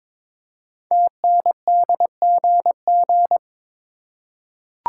Morse code, heard as a keyed tone, about 22 wpm, 700 Hz. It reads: TNDGG
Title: Code Groups - 5 Letters - 22wpm